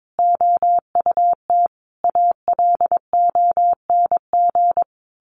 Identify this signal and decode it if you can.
Morse code, heard as a keyed tone, about 22 wpm, 700 Hz. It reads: OUT ALONG
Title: A Sound Of Thunder 22wpm